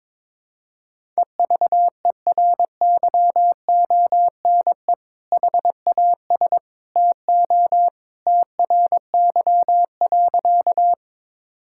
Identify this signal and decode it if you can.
Morse code, heard as a keyed tone, about 21 wpm, 700 Hz. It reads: EVERYONE HAS TO TRY.
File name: practice / 22wpm / Sentences with 2-Word Phrases - 22wpm